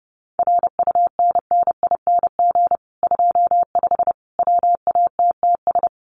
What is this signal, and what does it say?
Morse code, heard as a keyed tone, about 30 wpm, 700 Hz. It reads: RUNNING 25 WATTS
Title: Common QSO Phrases 30wpm